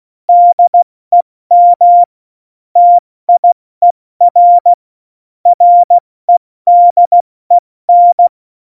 Morse code, heard as a keyed tone, about 16 wpm, spaced as for 14 wpm, 700 Hz. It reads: DEM TIER REDEN
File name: Morsekurs / Witze / Witz 23. Oktober 2018, 70 F80